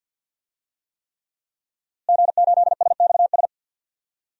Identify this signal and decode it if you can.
Morse code, heard as a keyed tone, about 50 wpm, 700 Hz. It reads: G8SXS
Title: DXCC-100 Call Signs - 5 Characters - Repeated 3x - 50wpm